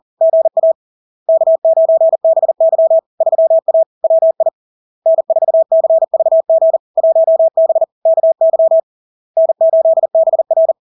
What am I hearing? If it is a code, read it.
Morse code, heard as a keyed tone, about 40 wpm, 650 Hz. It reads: GA K9BY 3A WI N4CVG 1B KY N8BR
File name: LARK-FD-2022-Log-in-CW-40wpm-650-Hz